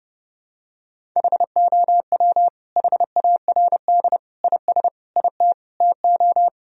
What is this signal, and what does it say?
Morse code, heard as a keyed tone, about 32 wpm, 700 Hz. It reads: HOW HARD IS IT TO
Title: Sentences from Top 1000 Words - Review Part 1 - 30wpm